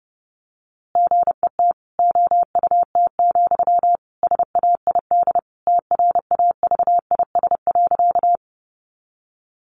Morse code, heard as a keyed tone, about 30 wpm, 700 Hz. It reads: GET OUT, SAID TRAVIS.
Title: A Sound Of Thunder 30wpm